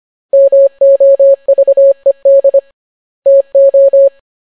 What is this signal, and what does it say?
Morse code, heard as a keyed tone, about 25 wpm, 550 Hz. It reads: MOVED TO